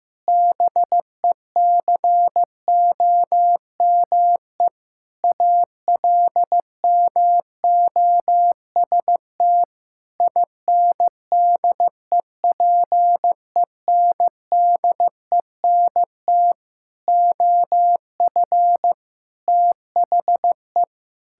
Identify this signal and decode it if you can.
Morse code, heard as a keyed tone, about 15 wpm, 700 Hz. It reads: BECOME ALMOST INDEPENDENT OF THE
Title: Sentences from Top 1000 Words - Review Part 6 - 15wpm